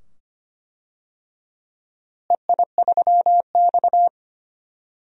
Morse code, heard as a keyed tone, about 25 wpm, 700 Hz. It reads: EI3X
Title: DXCC-100 Call Signs - 4 Characters - Repeated 2x - 25wpm